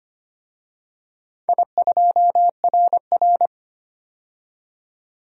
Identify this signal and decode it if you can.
Morse code, heard as a keyed tone, about 25 wpm, 700 Hz. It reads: I2RR
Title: DXCC-100 Call Signs - 4 Characters - 25wpm